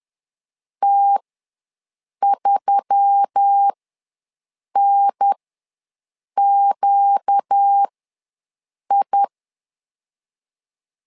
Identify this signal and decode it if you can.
Morse code, heard as a keyed tone, about 10 wpm, spaced as for 6 wpm, 800 Hz. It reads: T3NQI
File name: LEZIONI / 4 Q1S3 ws 10wpm 2437